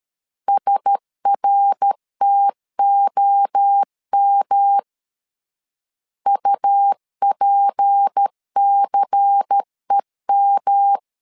Morse code, heard as a keyed tone, about 13 wpm, 800 Hz. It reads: SRTOM UPCEM
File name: CW / DET / 1 6 1500C S A 12wpm 3440